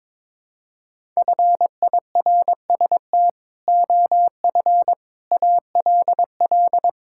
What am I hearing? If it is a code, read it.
Morse code, heard as a keyed tone, about 22 wpm, 700 Hz. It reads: FIRST OF ALL
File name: practice / 22wpm / Sentences with 3-Word Phrases - 22wpm